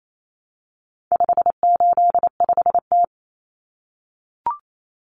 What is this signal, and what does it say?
Morse code, heard as a keyed tone, about 28 wpm, 700 Hz. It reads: 585T
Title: Common Cut Numbers - 4-Digit 28wpm